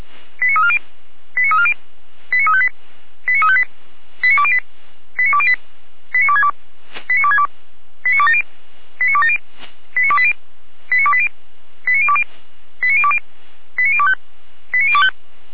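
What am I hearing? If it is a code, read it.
Five-tone selective call (ZVEI), heard as quick runs of five tones, tones of about 70 ms. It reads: 89340 89340 89358 89358 89208 89208 89262 89262 89270 89270 89280 89280 89020 89020 89025 89025
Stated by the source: ZVEI five-tone call received by SDR, recorded by Feuerwehr Orlamünde fire brigade.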